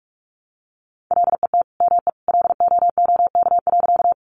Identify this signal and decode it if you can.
Morse code, heard as a keyed tone, about 44 wpm, 700 Hz. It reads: LET ME LOOK.
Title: Sentences from Top 300 Words 45wpm